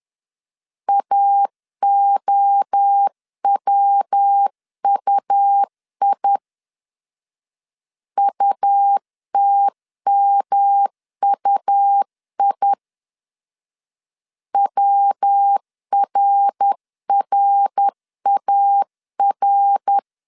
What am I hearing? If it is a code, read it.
Morse code, heard as a keyed tone, about 10 wpm, 800 Hz. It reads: AOWUI UTMUI WRRAR
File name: 10wpm